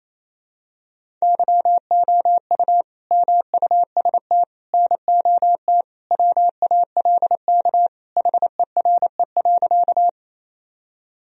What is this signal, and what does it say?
Morse code, heard as a keyed tone, about 28 wpm, 700 Hz. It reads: YOU MUST NOT WALK HERE.
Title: Sentences from Top 1000 Words - Review Part 1 - 28wpm